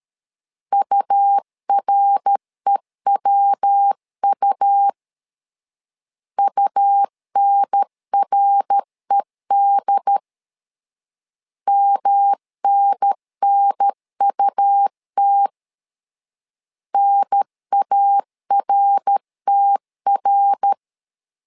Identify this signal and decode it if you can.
Morse code, heard as a keyed tone, about 13 wpm, 800 Hz. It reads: UREWU UNRED MNNUT NARTR
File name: LEZIONI / 3 MWRO / 3 MWR 12wpm 1711